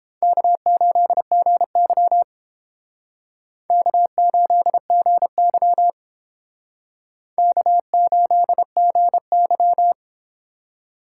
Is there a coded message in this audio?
Yes: Morse code from 0.2 to 9.9 s, heard as a keyed tone, about 30 wpm, 700 Hz. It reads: K8GY K8GY K8GY